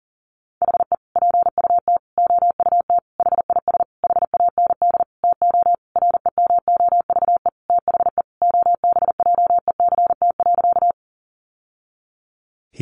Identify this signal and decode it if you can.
Morse code, heard as a keyed tone, about 40 wpm, 700 Hz. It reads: HE PUT OUT HIS HAND TO REMOVE THE OBJECT.
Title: Sentences from Top 800 Words 40wpm